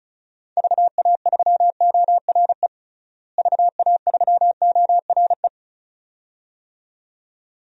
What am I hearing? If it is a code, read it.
Morse code, heard as a keyed tone, about 36 wpm, 700 Hz. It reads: VA3ORE VA3ORE
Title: DXCC-100 Call Signs - 6 Characters - Repeated 2x - 35wpm